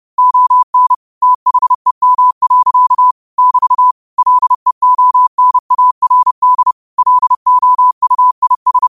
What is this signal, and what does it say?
Morse code, heard as a keyed tone, about 30 wpm, 1000 Hz. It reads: ON THEM. <BT> LEONARD LOUIS
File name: qotdfiles / 30wpm